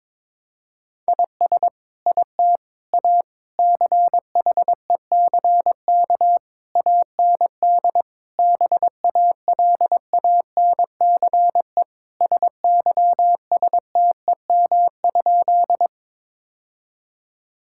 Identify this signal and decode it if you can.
Morse code, heard as a keyed tone, about 22 wpm, 700 Hz. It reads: IS IT A CHECK AND BALANCE SYSTEM?
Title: Sentences from Top 1000 Words - Review Part 6 - 22wpm